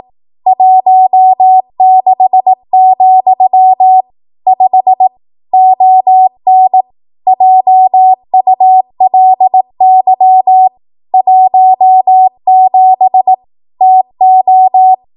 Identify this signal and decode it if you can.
Morse code, heard as a keyed tone, about 18 wpm, 750 Hz. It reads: 16, 5 ON JULY 17 TO
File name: Archive / 18 WPM / 18WPM